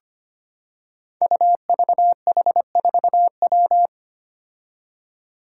Morse code, heard as a keyed tone, about 25 wpm, 700 Hz. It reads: UVH4W
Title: Five Challenging Letter-Number Combinations 25wpm